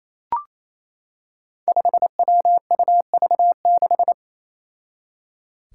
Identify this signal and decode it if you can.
Morse code, heard as a keyed tone, about 28 wpm, 700 Hz. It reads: 5WUV6
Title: Five Challenging Letter-Number Combinations 28wpm